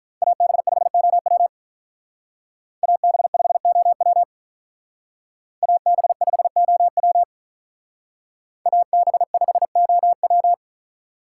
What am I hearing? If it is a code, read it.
Morse code, heard as a keyed tone, about 45 wpm, 700 Hz. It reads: AB5OW AB5OW AB5OW AB5OW